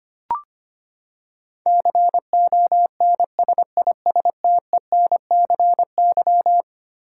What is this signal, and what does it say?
Morse code, heard as a keyed tone, about 25 wpm, 700 Hz. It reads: CONSISTENCY
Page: morse-code-ninja-practice-25wpm